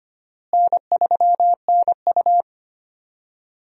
Morse code, heard as a keyed tone, about 25 wpm, 700 Hz. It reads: N3NU
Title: USA Call Signs - 4 Characters - Repeated 3x - 25wpm